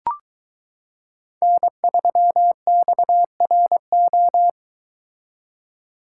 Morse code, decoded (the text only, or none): N3XRO